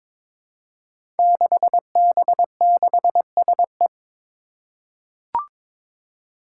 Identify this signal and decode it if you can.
Morse code, heard as a keyed tone, about 22 wpm, 700 Hz. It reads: 6B6SE
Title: Five Challenging Letter-Number Combinations 22wpm